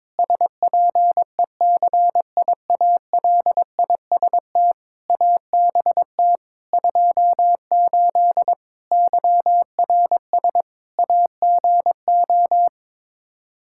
Morse code, heard as a keyed tone, about 22 wpm, 700 Hz. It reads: SPECIALIST ABT 28 YRS AGO
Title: Practice QSO - Exchange 3 - More Info - 22wpm